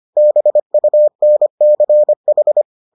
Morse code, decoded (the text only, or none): BUNCH